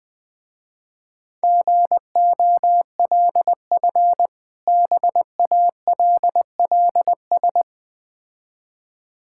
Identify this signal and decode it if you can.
Morse code, heard as a keyed tone, about 21 wpm, 690 Hz. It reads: GOLF BALLS
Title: Difficult Words with Hints - 20wpm